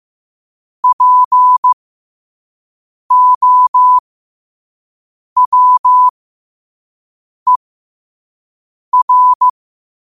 Morse code, heard as a keyed tone, about 15 wpm, spaced as for 5 wpm, 1000 Hz. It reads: POWER
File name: qotdfiles / 5wpm